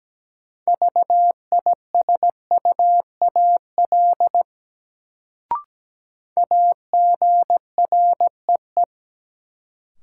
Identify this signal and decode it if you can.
Morse code, heard as a keyed tone, about 17 wpm, 700 Hz. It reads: VISUAL AGREE
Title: Sentences from Top 1000 Words - Review Part 6 - New Word Drill - 17wpm